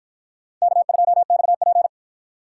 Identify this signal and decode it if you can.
Morse code, heard as a keyed tone, about 46 wpm, 700 Hz. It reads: K2XP